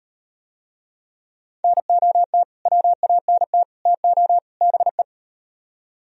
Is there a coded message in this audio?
Yes: Morse code from 1.6 to 5.0 s, heard as a keyed tone, about 38 wpm, 700 Hz. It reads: NOT WANT TO BE